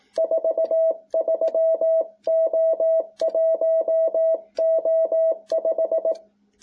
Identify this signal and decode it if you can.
Morse code, heard as a keyed tone, about 18 wpm, 660 Hz. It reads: 43O1O5